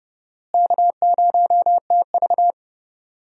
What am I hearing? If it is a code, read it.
Morse code, heard as a keyed tone, about 30 wpm, 700 Hz. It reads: K0TV
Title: USA Call Signs - 4 Characters - Repeated 3x - 30wpm